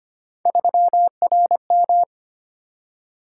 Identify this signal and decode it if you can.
Morse code, heard as a keyed tone, about 25 wpm, 700 Hz. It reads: 3RM
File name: Three Letter-Number Combinations - 25wpm